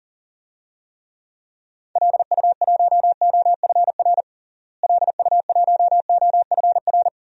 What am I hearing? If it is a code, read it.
Morse code, heard as a keyed tone, about 37 wpm, 700 Hz. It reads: LU1OFR LU1OFR
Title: DXCC-100 Call Signs - 6 Characters - Repeated 3x - 40wpm